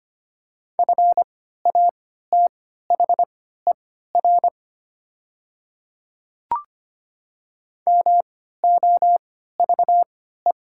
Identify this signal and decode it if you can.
Morse code, heard as a keyed tone, about 25 wpm, spaced as for 14 wpm, 700 Hz. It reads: FATHER MOVE
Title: 3x Character Spacing - Top 200 Words - 25wpm